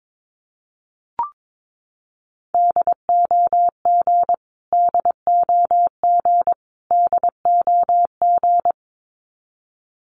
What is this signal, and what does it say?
Morse code, heard as a keyed tone, about 22 wpm, 700 Hz. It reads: DOG DOG DOG
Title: Top 500 Words Repeated 22wpm